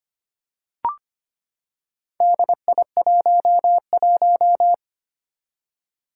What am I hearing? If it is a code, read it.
Morse code, heard as a keyed tone, about 25 wpm, 700 Hz. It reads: DI11
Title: Four Challenging Letter-Number Combinations 25wpm